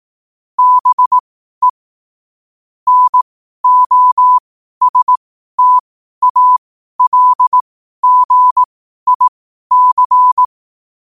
Morse code, heard as a keyed tone, about 18 wpm, spaced as for 12 wpm, 1000 Hz. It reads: BE NOSTALGIC
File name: qotdfiles / 12wpm